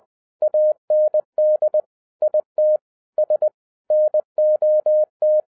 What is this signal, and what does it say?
Morse code, heard as a keyed tone, about 20 wpm, 610 Hz. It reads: AND IT S NOT